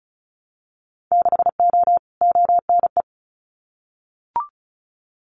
Morse code, heard as a keyed tone, about 36 wpm, 700 Hz. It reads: 6O ONE
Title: Winter Field Day Exchanges Encore 35wpm